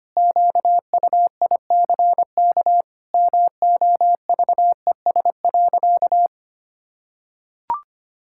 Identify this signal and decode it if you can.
Morse code, heard as a keyed tone, about 25 wpm, 700 Hz. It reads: QUICK MOVES.
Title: Pangram Challenge v2 - 25wpm